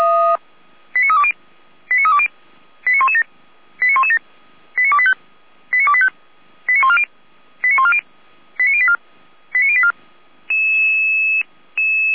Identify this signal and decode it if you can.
Five-tone selective call (ZVEI), heard as quick runs of five tones, tones of about 70 ms. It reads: 89320 89320 89117 89117 89285 89285 89140 89140 89084 89084
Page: SDR/ZVEI